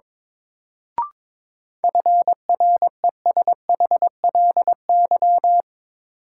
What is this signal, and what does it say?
Morse code, heard as a keyed tone, about 22 wpm, 700 Hz. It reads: FRESHLY
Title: Sets of 5 Words Encore - New Word Drill - 22wpm